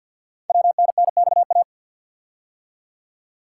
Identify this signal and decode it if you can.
Morse code, heard as a keyed tone, about 48 wpm, 700 Hz. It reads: WNNXA
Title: Code Groups - 5 Letters - 50wpm